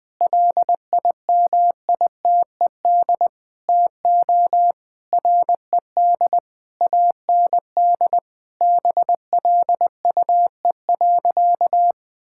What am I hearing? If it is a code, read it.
Morse code, heard as a keyed tone, about 20 wpm, 700 Hz. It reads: LIMITED TO RED AND BLUE.
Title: Sentences with 6-Word Phrases - 20wpm